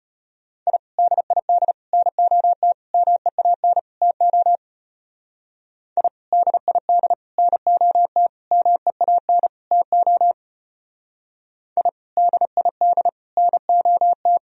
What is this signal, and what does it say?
Morse code, heard as a keyed tone, about 29 wpm, 700 Hz. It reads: I DID NOT MEAN TO I DID NOT MEAN TO I DID NOT
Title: Speed-Racing - Sets of 5 Words 25wpm